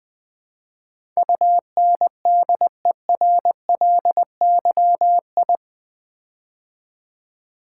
Morse code, heard as a keyed tone, about 20 wpm, 700 Hz. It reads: UNDERLYI
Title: Build A Word - 4 Letters At A Time - 10-Letter Words - 20wpm